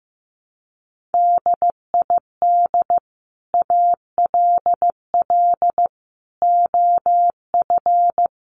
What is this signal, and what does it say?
Morse code, heard as a keyed tone, about 15 wpm, 700 Hz. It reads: DID ALL OF
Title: A Sound Of Thunder 15wpm